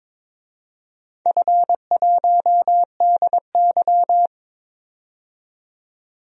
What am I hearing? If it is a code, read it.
Morse code, heard as a keyed tone, about 23 wpm, 700 Hz. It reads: F1DY
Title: DXCC-100 Call Signs - 4 Characters - 22wpm